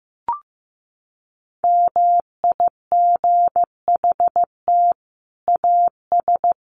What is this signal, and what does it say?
Morse code, heard as a keyed tone, about 15 wpm, 700 Hz. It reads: MIGHT AS
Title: Sets of 5 Words Encore2 15wpm